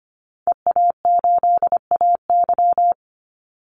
Morse code, heard as a keyed tone, about 25 wpm, 700 Hz. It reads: EA8AY
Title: Call Signs Repeated 3x - 25wpm